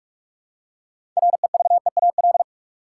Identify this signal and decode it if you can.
Morse code, heard as a keyed tone, about 46 wpm, 700 Hz. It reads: REVEAL
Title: Six Letter Words - 45wpm